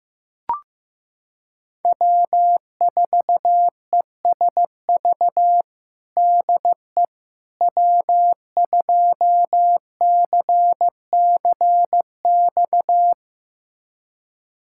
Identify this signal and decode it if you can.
Morse code, heard as a keyed tone, about 15 wpm, 700 Hz. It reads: W4ESV DE W2CCX
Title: USA Callsign DE Callsign - 5 Characters - 15wpm